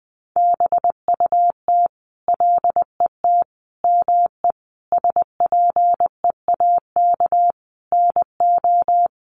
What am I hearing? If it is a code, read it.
Morse code, heard as a keyed tone, about 20 wpm, 700 Hz. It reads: BUT LET ME SPEAK NO
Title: Sentences from Top 1100 Words 20wpm